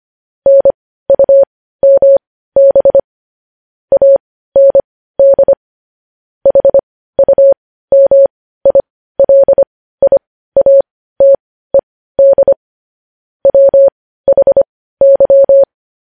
Morse code, heard as a keyed tone, about 25 wpm, spaced as for 15 wpm, 550 Hz. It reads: NUMB AND HUMILIATED WHY
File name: podcast / News headlines-1x-25x15